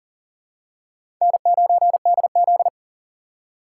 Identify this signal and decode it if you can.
Morse code, heard as a keyed tone, about 40 wpm, 700 Hz. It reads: N9DZ
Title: USA Call Signs - 4 Characters - 6x Character Spacing - 40wpm